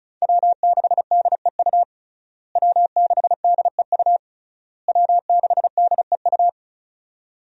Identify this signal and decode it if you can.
Morse code, heard as a keyed tone, about 35 wpm, 700 Hz. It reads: W6DEU W6DEU W6DEU
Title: USA Call Signs - 5 Characters - Repeated 3x - 35wpm